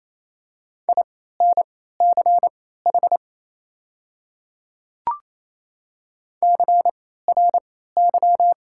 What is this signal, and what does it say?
Morse code, heard as a keyed tone, about 28 wpm, spaced as for 16 wpm, 700 Hz. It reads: INCH CRY
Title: 3x Character Spacing - Top 500 Words - 28wpm